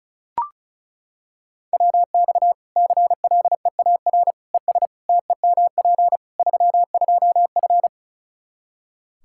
Morse code, heard as a keyed tone, about 36 wpm, 700 Hz. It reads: WX CLEAR ES TEMP 32F
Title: Practice QSO - Exchange 2 - Info - 35wpm